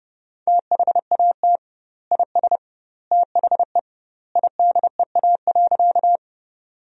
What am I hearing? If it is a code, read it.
Morse code, heard as a keyed tone, about 30 wpm, 700 Hz. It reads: THAT IS THE IDEA.
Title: 2x Word-Spacing - Sentences from Top 300 Words - 30wpm